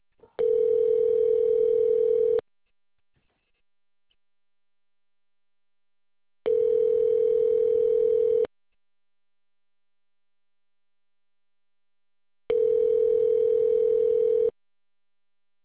Ringback tone (North America), 440 + 480 Hz.